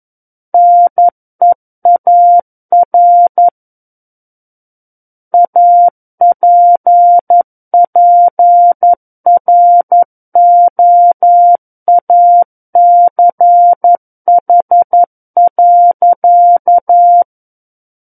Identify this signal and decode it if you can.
Morse code, heard as a keyed tone, about 11 wpm, 700 Hz. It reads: NEAR APPROACH.
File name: Book One / 02-Chapter 2 - 11 WPM